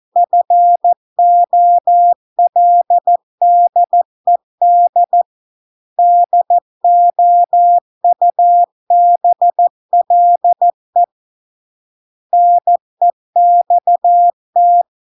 Morse code, heard as a keyed tone, about 14 wpm, 700 Hz. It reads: FOLDED DOUBLE NEXT